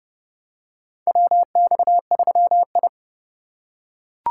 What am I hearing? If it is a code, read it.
Morse code, heard as a keyed tone, about 30 wpm, 700 Hz. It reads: WX3I
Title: USA Call Signs - 4 Characters - 3x Character Spacing - 30wpm